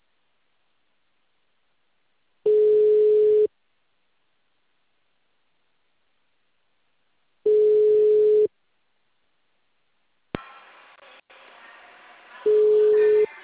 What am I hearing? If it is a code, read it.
Ringback tone (Europe, 425 Hz).